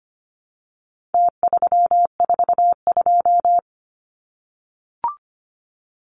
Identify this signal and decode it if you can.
Morse code, heard as a keyed tone, about 25 wpm, 700 Hz. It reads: T342
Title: Numbers - Contest Style - 4-Digit 25wpm